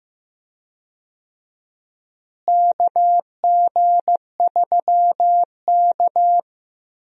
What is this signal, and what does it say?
Morse code, heard as a keyed tone, about 15 wpm, 700 Hz. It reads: KG3K
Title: USA Call Signs - 4 Characters - Repeated 2x - 15wpm